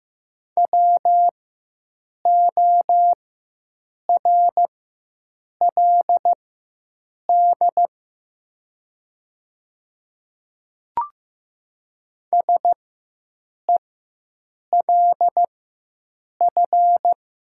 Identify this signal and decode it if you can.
Morse code, heard as a keyed tone, about 15 wpm, spaced as for 7 wpm, 700 Hz. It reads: WORLD SELF
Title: 4x Character Spacing - Top 300 Words - 15wpm